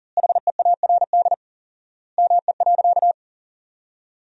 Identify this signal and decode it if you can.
Morse code, heard as a keyed tone, about 40 wpm, 700 Hz. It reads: HEARD ME.
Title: 4x Word-Spacing - Sentences from Top 400 Words - 40wpm